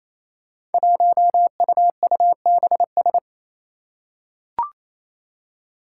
Morse code, heard as a keyed tone, about 28 wpm, 700 Hz. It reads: 1UUBS
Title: Five Challenging Letter-Number Combinations 28wpm